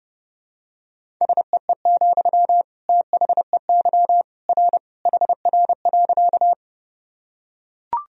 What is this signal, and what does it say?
Morse code, heard as a keyed tone, about 30 wpm, 700 Hz. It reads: SEE, THEY R HR.
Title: Sentences from Top 200 Words and Abbreviations 30wpm